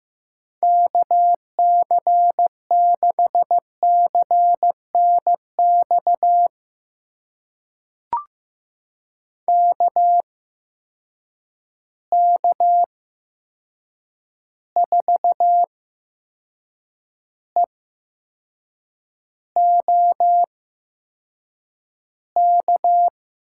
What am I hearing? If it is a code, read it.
Morse code, heard as a keyed tone, about 15 wpm, spaced as for 6 wpm, 700 Hz. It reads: KC6CNX KK4EOK
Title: USA Call Signs - 6 Characters - 8x Character Spacing - 15wpm